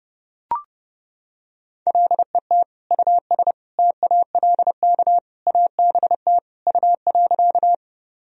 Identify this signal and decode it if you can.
Morse code, heard as a keyed tone, about 30 wpm, 700 Hz. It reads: LET US TALK ABT U.
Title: Sentences from Top 400 Words and Abbreviations 30wpm